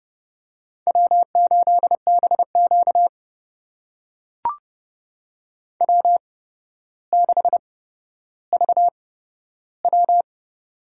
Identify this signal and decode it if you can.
Morse code, heard as a keyed tone, about 30 wpm, spaced as for 13 wpm, 700 Hz. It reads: W8BQ W6VW
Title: USA Call Signs - 8x Character Spacing - 30wpm